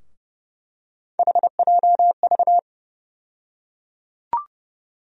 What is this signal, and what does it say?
Morse code, heard as a keyed tone, about 31 wpm, 700 Hz. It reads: HJV
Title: USA Call Sign Postfixes - 3 Letters 30wpm